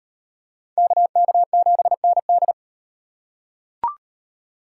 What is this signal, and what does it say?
Morse code, heard as a keyed tone, about 38 wpm, 700 Hz. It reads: KK7ND